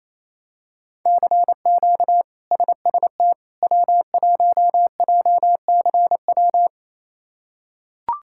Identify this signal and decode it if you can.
Morse code, heard as a keyed tone, about 28 wpm, 700 Hz. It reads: CQ SST W1JCW
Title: K1USN SST QSO Practice - 28wpm